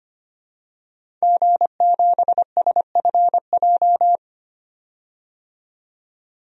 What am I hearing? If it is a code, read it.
Morse code, heard as a keyed tone, about 25 wpm, 700 Hz. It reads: G7SFJ
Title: DXCC-100 Call Signs - 5 Characters - 25wpm